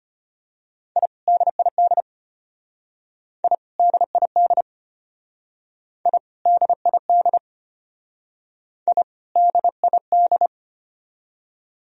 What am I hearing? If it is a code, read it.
Morse code, heard as a keyed tone, about 32 wpm, 700 Hz. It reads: I DID I DID I DID I DID